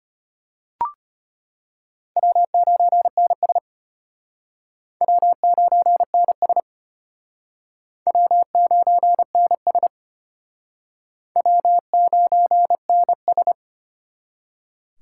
Morse code, heard as a keyed tone, about 32 wpm, 700 Hz. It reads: W9NS W9NS W9NS W9NS